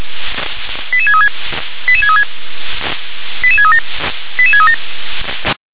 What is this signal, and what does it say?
Five-tone selective call (ZVEI), heard as quick runs of five tones, tones of about 70 ms. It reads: 88636 88636 88637 88637